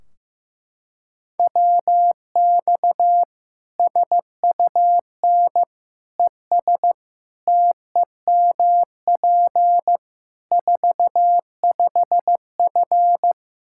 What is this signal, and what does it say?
Morse code, heard as a keyed tone, about 16 wpm, 700 Hz. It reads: WX SUN ES TEMP 45F
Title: Practice QSO - Exchange 2 - Info - 15wpm